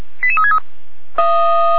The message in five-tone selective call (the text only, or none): 88363